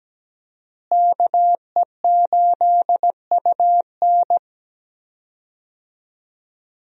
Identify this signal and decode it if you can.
Morse code, heard as a keyed tone, about 17 wpm, 700 Hz. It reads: KE8UN
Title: USA Call Signs - 5 Characters - Repeated 2x - 17wpm